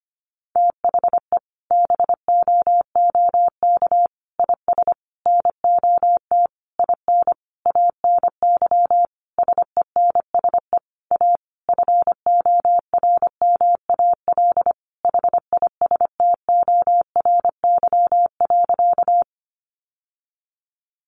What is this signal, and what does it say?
Morse code, heard as a keyed tone, about 25 wpm, 700 Hz. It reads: THE BOOK IS NOT IN ANY SENSE A FORMAL HISTORY.
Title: Sentences from Top 1300 Words 25wpm